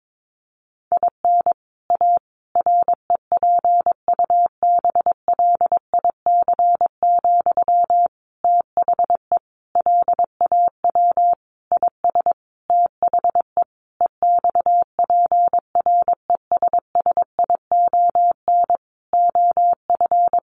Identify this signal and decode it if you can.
Morse code, heard as a keyed tone, about 22 wpm, 700 Hz. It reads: IN A REPUBLIC, THE LAW IS THE EXPRESSION OF
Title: Sentences from Top 1500 Words 22wpm